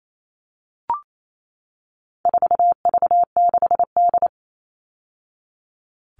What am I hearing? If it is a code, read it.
Morse code, heard as a keyed tone, about 28 wpm, 700 Hz. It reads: 4V6D
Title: All Cut Numbers - 4-Digit 28wpm